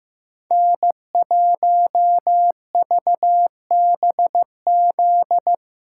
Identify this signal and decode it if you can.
Morse code, heard as a keyed tone, about 15 wpm, 700 Hz. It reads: N1VBZ
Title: USA Callsign DE Callsign - 5 Characters - 15wpm